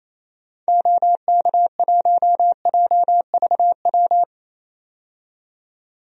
Morse code, heard as a keyed tone, about 28 wpm, 700 Hz. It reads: OK1JVW
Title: DXCC-100 Call Signs - 6 Characters - 28wpm